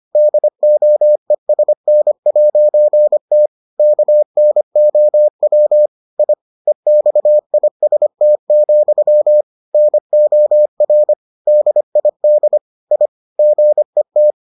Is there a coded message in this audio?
Yes: Morse code from 0.2 to 14.3 s, heard as a keyed tone, about 25 wpm, 600 Hz. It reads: DOESN'T KNOW I EXIST, NOR DID I GET